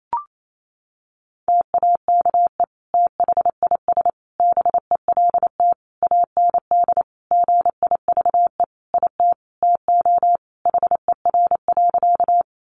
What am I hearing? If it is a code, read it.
Morse code, heard as a keyed tone, about 28 wpm, 700 Hz. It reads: TAKE THIS BELT AND GIVE IT TO HER.
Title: Sentences from Top 1400 Words 28wpm